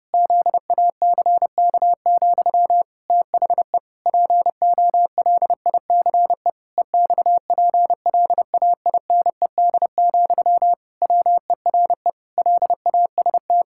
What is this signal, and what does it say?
Morse code, heard as a keyed tone, about 30 wpm, 700 Hz. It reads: ZACK, THE POLICE EXPLAINED, WERE LAST
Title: Pangram Challenge v2 - 30wpm